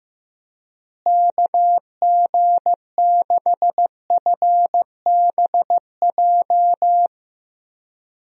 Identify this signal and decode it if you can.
Morse code, heard as a keyed tone, about 15 wpm, 700 Hz. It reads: KG6FBJ
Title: USA Call Signs - 6 Characters - Repeated 2x - 15wpm